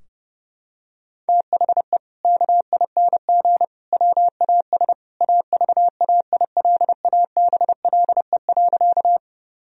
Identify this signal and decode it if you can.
Morse code, heard as a keyed tone, about 30 wpm, 700 Hz. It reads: THE KING WAS AVAILABLE.
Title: Sentences from Top 1000 Words - Review Part 3 - 30wpm